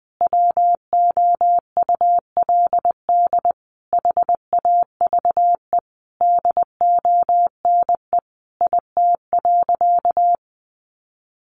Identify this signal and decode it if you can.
Morse code, heard as a keyed tone, about 20 wpm, 700 Hz. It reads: WOULD HAVE DONE IT.
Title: Sentences from Top 1200 Words 20wpm